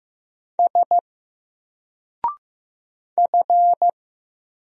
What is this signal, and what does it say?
Morse code, heard as a keyed tone, about 15 wpm, 700 Hz. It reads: S F
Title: Single Character (Letter:Number:Punctuation:Prosign) 15wpm